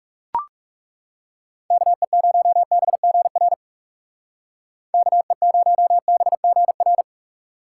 Morse code, heard as a keyed tone, about 40 wpm, 700 Hz. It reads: KE0BGR KE0BGR